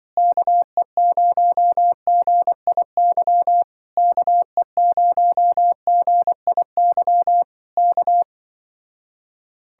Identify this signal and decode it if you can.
Morse code, heard as a keyed tone, about 24 wpm, 700 Hz. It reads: KE0GIY KE0GIY K
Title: Speed-Racing Common QSO Phrases - 20wpm